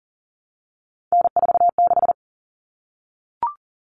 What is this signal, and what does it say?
Morse code, heard as a keyed tone, about 40 wpm, 700 Hz. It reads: N46
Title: Common Cut Numbers - 3-Digit 40wpm